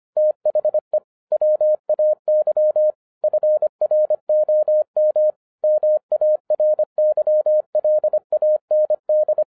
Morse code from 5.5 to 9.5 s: MARYLAND